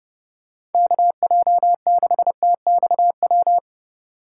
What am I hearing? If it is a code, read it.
Morse code, heard as a keyed tone, about 30 wpm, 700 Hz. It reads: KJ6TXW
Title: USA Call Signs - 8x Character Spacing - 30wpm